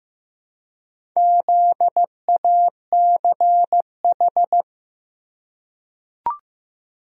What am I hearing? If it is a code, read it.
Morse code, heard as a keyed tone, about 15 wpm, 700 Hz. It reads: ZACH
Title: USA Common Names - Max 4 Letters - 15wpm